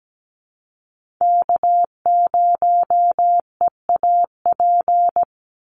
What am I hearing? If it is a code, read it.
Morse code, heard as a keyed tone, about 17 wpm, 700 Hz. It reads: K0EAP
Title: NA SOTA Chaser Callsigns 17wpm